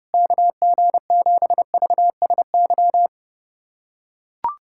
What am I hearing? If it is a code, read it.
Morse code, heard as a keyed tone, about 32 wpm, 700 Hz. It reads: KG7VSY
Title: USA Callsign DE Callsign 30wpm